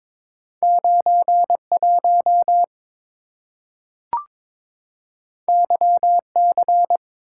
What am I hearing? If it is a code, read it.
Morse code, heard as a keyed tone, about 22 wpm, 690 Hz. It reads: 91 YC
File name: practice / 22wpm / Two Letter-Number Combinations - 22wpm